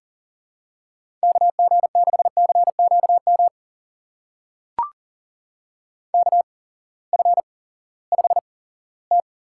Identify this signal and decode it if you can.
Morse code, heard as a keyed tone, about 40 wpm, spaced as for 20 wpm, 700 Hz. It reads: KG6CQM KF5T
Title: USA Call Signs - 8x Character Spacing - 40wpm